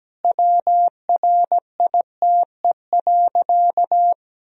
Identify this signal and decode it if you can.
Morse code, heard as a keyed tone, about 17 wpm, 700 Hz. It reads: WRITE.
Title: Sentences from Top 1000 Words - Review Part 1 - 17wpm